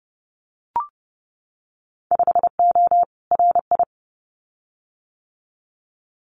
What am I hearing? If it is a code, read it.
Morse code, heard as a keyed tone, about 30 wpm, 700 Hz. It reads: 5O RI
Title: Winter Field Day Exchanges Encore 30wpm